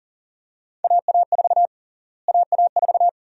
Morse code, heard as a keyed tone, about 37 wpm, 700 Hz. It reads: AA4 AA4
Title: USA Call Sign Prefixes - Repeated 2x - 40wpm